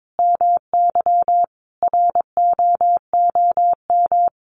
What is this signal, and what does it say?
Morse code, heard as a keyed tone, about 22 wpm, 700 Hz. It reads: MY ROOM
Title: Sentences from Top 1300 Words 22wpm